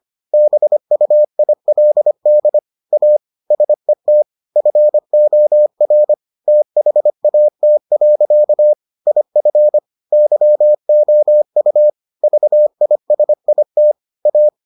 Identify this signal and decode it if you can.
Morse code, heard as a keyed tone, about 25 wpm, 600 Hz. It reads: BUILD A SET FOR THAT. IF YOU VISIT A